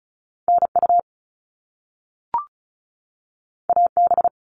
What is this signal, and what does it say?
Morse code, heard as a keyed tone, about 36 wpm, 700 Hz. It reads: NU AB
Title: Canadian Provinces and Territories 35wpm